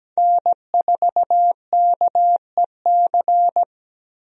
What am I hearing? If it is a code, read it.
Morse code, heard as a keyed tone, about 17 wpm, 700 Hz. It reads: N4KEC
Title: USA Call Signs - 5 Characters - Repeated 3x - 17wpm